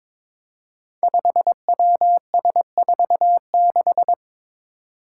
Morse code, heard as a keyed tone, about 22 wpm, 700 Hz. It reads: 5WS46
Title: Five Challenging Letter-Number Combinations 22wpm